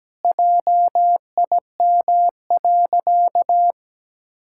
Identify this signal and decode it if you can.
Morse code, heard as a keyed tone, about 17 wpm, 700 Hz. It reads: JIM.
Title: Pangram Challenge v2 - 17wpm